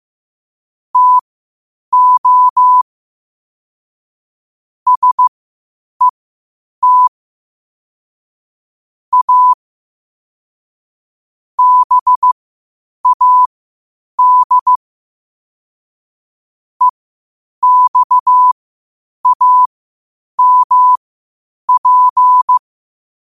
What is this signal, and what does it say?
Morse code, heard as a keyed tone, about 15 wpm, spaced as for 8 wpm, 1000 Hz. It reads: TO SET A BAD EXAMP
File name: qotdfiles / 8wpm